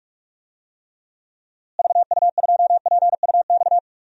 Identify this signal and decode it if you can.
Morse code, heard as a keyed tone, about 46 wpm, 700 Hz. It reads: VU2PUX
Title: DXCC-100 Call Signs - 6 Characters - Repeated 3x - 45wpm